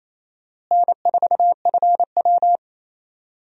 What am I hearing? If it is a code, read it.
Morse code, heard as a keyed tone, about 27 wpm, 700 Hz. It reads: N4FW
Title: USA Call Signs - 4 Characters - Repeated 3x - 28wpm